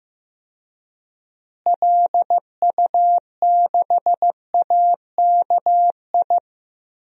Morse code, heard as a keyed tone, about 16 wpm, 700 Hz. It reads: LU6AKI
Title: DXCC-100 Call Signs - 6 Characters - Repeated 3x - 15wpm